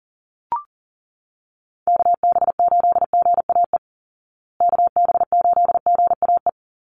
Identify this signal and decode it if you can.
Morse code, heard as a keyed tone, about 40 wpm, 700 Hz. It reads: KB8GAE KB8GAE
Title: Call Signs Repeated 2x - 40wpm